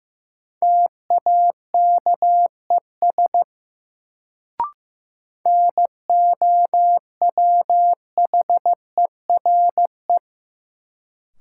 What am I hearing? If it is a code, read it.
Morse code, heard as a keyed tone, about 15 wpm, 700 Hz. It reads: TAKES NOWHERE
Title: Sets of 5 Words Encore - New Word Drill - 15wpm